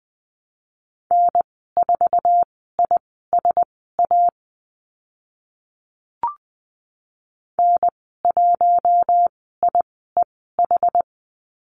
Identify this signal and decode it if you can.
Morse code, heard as a keyed tone, about 20 wpm, spaced as for 14 wpm, 700 Hz. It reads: N4ISA N1IEH
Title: 069 Lesson 12g - Letter d - Call Signs 20wpm